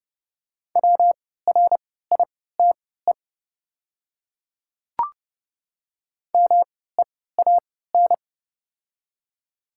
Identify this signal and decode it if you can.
Morse code, heard as a keyed tone, about 30 wpm, spaced as for 17 wpm, 690 Hz. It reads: WRITE MEAN
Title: 3x Character Spacing - Top 200 Words - 30wpm